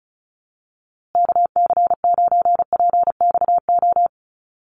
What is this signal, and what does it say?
Morse code, heard as a keyed tone, about 34 wpm, 700 Hz. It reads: KC9PXO
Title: USA Call Signs Repeated 3x 35wpm